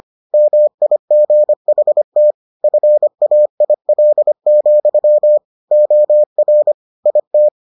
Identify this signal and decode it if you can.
Morse code, heard as a keyed tone, about 25 wpm, 600 Hz. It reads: MIGHT FAIL, OR IT